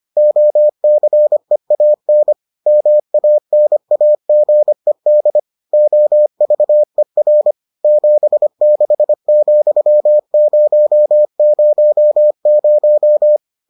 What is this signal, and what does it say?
Morse code, heard as a keyed tone, about 25 wpm, 600 Hz. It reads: OCEAN MANAGED OVER 76,000